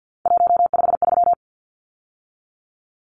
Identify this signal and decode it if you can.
Morse code, heard as a keyed tone, about 50 wpm, 700 Hz. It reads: 153
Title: Numbers - 3-Digit 50wpm